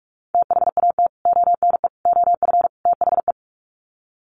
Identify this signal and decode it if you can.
Morse code, heard as a keyed tone, about 44 wpm, 700 Hz. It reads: THAT ONE OF THE
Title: Sets of 4 Words 45wpm